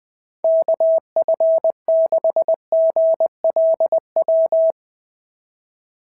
Morse code, heard as a keyed tone, about 21 wpm, 690 Hz. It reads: KF6GLW
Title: SOTA QSO Practice 20wpm